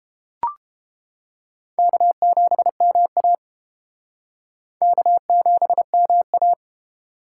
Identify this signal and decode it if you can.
Morse code, heard as a keyed tone, about 30 wpm, 700 Hz. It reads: K7MA K7MA